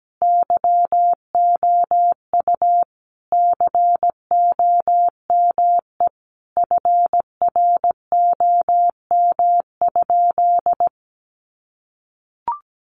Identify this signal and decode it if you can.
Morse code, heard as a keyed tone, about 17 wpm, 700 Hz. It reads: YOU COME FROM?
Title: Sentences from Top 200 Words 17wpm